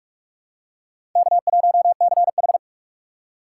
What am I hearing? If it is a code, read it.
Morse code, heard as a keyed tone, about 46 wpm, 700 Hz. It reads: K1CH
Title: USA Call Signs - 4 Characters - 6x Character Spacing - 45wpm